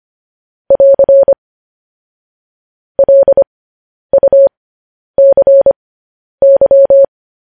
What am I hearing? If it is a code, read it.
Morse code, heard as a keyed tone, about 25 wpm, spaced as for 10 wpm, 550 Hz. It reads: <AR> LUCY
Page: news headlines in morse code at 10 wpm